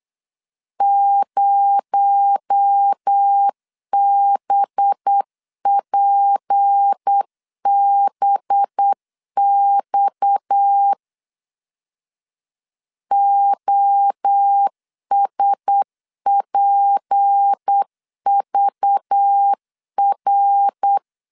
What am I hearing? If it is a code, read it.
Morse code, heard as a keyed tone, about 8 wpm, 800 Hz. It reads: 0BPBX OSPVR
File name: LEZIONI / 8 YJ45 / 8 YJ45 8wpm 3214